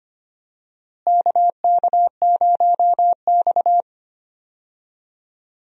Morse code, heard as a keyed tone, about 25 wpm, 700 Hz. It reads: KK0X
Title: USA Call Signs - 4 Characters - Repeated 3x - 25wpm